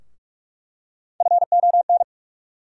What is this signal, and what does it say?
Morse code, heard as a keyed tone, about 43 wpm, 700 Hz. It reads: FON
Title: USA Call Sign Postfixes - 3 Letters 45wpm